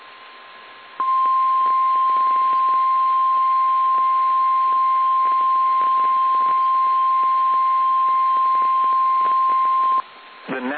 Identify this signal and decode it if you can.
Emergency Alert System (SAME) tones (the 1050 Hz attention tone), with speech. It is heard from 0.9 to 10.1 s.
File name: eas / KSDZKDJL / KSDZKDJL